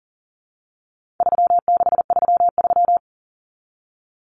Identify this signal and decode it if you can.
Morse code, heard as a keyed tone, about 40 wpm, 700 Hz. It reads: 3633
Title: Numbers - 4-Digit 40wpm